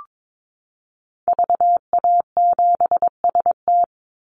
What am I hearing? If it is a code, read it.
Morse code, heard as a keyed tone, about 22 wpm, 700 Hz. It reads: VA7ST
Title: Call Signs Repeated 3x - 22wpm